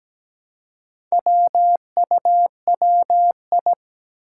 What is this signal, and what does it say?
Morse code, heard as a keyed tone, about 17 wpm, 700 Hz. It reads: WUWI
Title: Four Challenging Letter-Number Combinations 17wpm